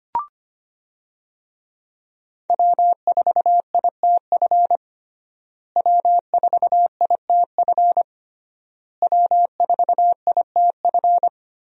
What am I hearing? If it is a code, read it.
Morse code, heard as a keyed tone, about 25 wpm, 700 Hz. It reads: W4ITF W4ITF W4ITF
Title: USA Call Signs - 5 Characters - Repeated 3x - 25wpm